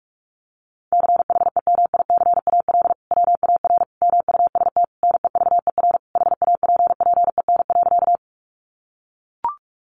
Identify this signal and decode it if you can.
Morse code, heard as a keyed tone, about 44 wpm, 700 Hz. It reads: CHEMICAL WAR MUST NEVER HAPPEN.
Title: Sentences from Top 1200 Words 45wpm